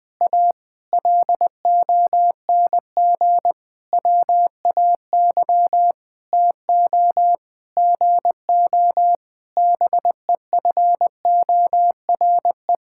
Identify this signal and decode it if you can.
Morse code, heard as a keyed tone, about 21 wpm, 700 Hz. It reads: A LONG WAY TO GO BEFORE
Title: Sentences with 9-Word Phrases - 20wpm